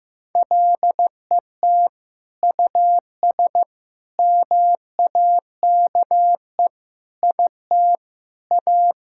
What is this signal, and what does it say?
Morse code, heard as a keyed tone, about 15 wpm, 700 Hz. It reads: LET US MAKE IT A
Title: Sentences from Top 1600 Words 15wpm